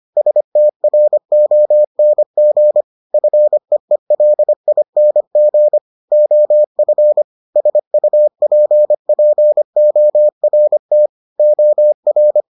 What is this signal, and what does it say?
Morse code, heard as a keyed tone, about 25 wpm, 600 Hz. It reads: STRONG FEELING OF SUPPORT OR